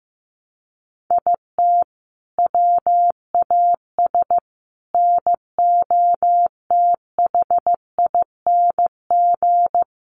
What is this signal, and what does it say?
Morse code, heard as a keyed tone, about 15 wpm, 700 Hz. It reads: IT WAS NOTHING
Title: Sentences from Top 1300 Words 15wpm